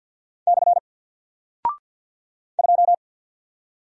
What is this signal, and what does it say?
Morse code, heard as a keyed tone, about 48 wpm, 700 Hz. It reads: / 2
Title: Single Character (Letter:Number:Punctuation:Prosign) Alt 50wpm